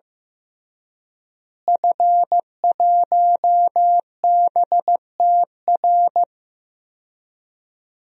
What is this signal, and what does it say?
Morse code, heard as a keyed tone, about 15 wpm, 700 Hz. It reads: F1BTR
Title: DXCC-100 Call Signs - 5 Characters - Repeated 3x - 15wpm